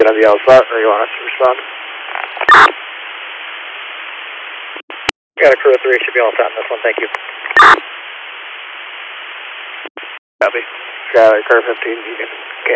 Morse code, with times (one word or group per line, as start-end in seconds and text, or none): none